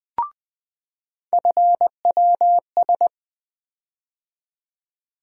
Morse code, heard as a keyed tone, about 20 wpm, 700 Hz. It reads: FWS
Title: USA Call Sign Postfixes - 3 Letters 20wpm